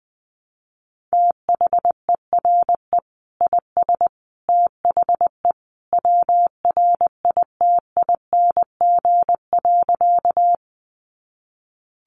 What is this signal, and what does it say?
Morse code, heard as a keyed tone, about 20 wpm, 700 Hz. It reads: THERE IS THE WRITING.
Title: Sentences from Top 800 Words 20wpm